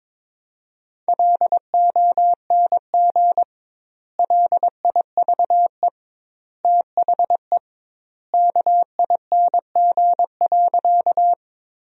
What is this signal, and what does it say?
Morse code, heard as a keyed tone, about 21 wpm, 700 Hz. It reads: LONG LIVE THE KING.
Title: 2x Word-Spacing - Sentences from Top 400 Words - 22wpm